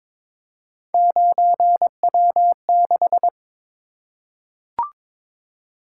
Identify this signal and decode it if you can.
Morse code, heard as a keyed tone, about 22 wpm, 700 Hz. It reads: 9W6
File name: practice / 22wpm / Three Letter-Number Combinations - 22wpm